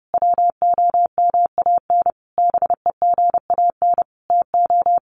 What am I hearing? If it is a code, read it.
Morse code, heard as a keyed tone, about 30 wpm, 700 Hz. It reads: WOMAN BEGAN TO
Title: Sentences from Top 1100 Words 30wpm